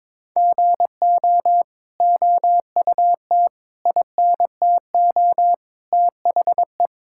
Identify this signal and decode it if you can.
Morse code, heard as a keyed tone, about 23 wpm, 700 Hz. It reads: GO OUT INTO THE
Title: Sentences from Top 1000 Words - Review Part 5 - 22wpm